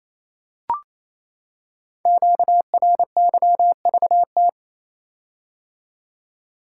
Morse code, heard as a keyed tone, about 29 wpm, 700 Hz. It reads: QRYVT
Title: Code Groups - 5 Letter:Numbers - 28wpm